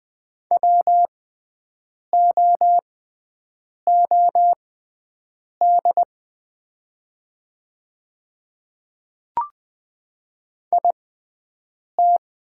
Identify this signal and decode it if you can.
Morse code, heard as a keyed tone, about 20 wpm, spaced as for 7 wpm, 700 Hz. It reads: WOOD IT
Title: 6x Character Spacing - Top 300 Words - 20wpm